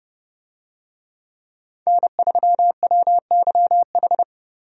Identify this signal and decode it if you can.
Morse code, heard as a keyed tone, about 28 wpm, 700 Hz. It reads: N3WYH
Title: USA Call Signs - 5 Characters - Repeated 3x - 30wpm